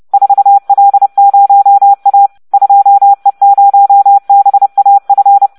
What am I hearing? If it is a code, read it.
Morse code, heard as a keyed tone, about 30 wpm, 800 Hz. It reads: 4L0A 2E0BAF